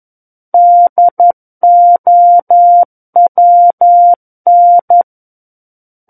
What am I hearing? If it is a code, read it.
Morse code, heard as a keyed tone, about 11 wpm, 700 Hz. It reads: DOWN
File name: Book One / 02-Chapter 2 - 11 WPM